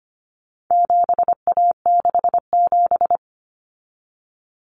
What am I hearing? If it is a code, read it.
Morse code, heard as a keyed tone, about 25 wpm, 700 Hz. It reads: 7A67
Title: Common Cut Numbers - 4-Digit 25wpm